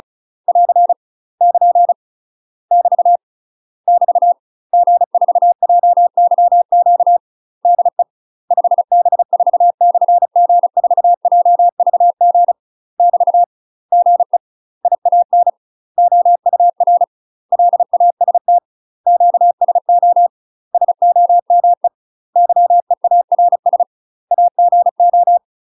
Morse code, heard as a keyed tone, about 36 wpm, 700 Hz. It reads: <AR> <KN> <BT> <BT> G4JYQ DE 5B4/G4JVG <BT> GE IAN OUR LAST QSO SOME YEARS AGO
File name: UK MASTER QSOs TRAINING/35WPM